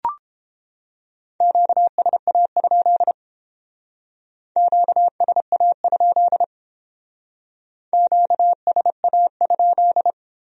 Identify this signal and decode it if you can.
Morse code, heard as a keyed tone, about 31 wpm, 700 Hz. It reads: QSA? QSA? QSA?